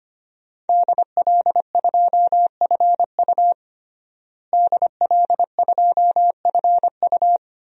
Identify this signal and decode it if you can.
Morse code, heard as a keyed tone, about 25 wpm, 700 Hz. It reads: DL2FU DL2FU
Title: DXCC-100 Call Signs - 5 Characters - Repeated 2x - 25wpm